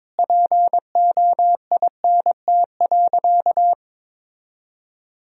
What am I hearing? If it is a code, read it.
Morse code, heard as a keyed tone, about 22 wpm, 700 Hz. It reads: POINT.
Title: Sentences from Top 1000 Words - Review Part 1 - 22wpm